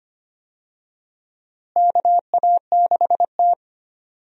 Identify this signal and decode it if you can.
Morse code, heard as a keyed tone, about 25 wpm, 700 Hz. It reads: KA6T
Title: USA Call Signs - 4 Characters - Repeated 2x - 25wpm